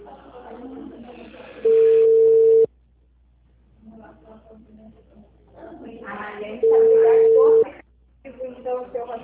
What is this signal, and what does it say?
Ringback tone (China, 450 Hz), with speech.